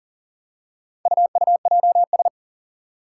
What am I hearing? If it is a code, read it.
Morse code, heard as a keyed tone, about 40 wpm, 700 Hz. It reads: UUJS
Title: Four Challenging Letter-Number Combinations 40wpm